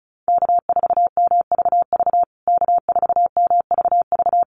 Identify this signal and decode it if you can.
Morse code, heard as a keyed tone, about 34 wpm, 700 Hz. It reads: K4MVV K4MVV
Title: USA Call Signs Repeated 3x 35wpm